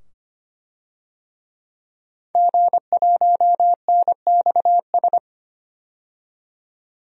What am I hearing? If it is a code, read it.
Morse code, heard as a keyed tone, about 25 wpm, 700 Hz. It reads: G1NXS
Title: DXCC-100 Call Signs - 5 Characters - 25wpm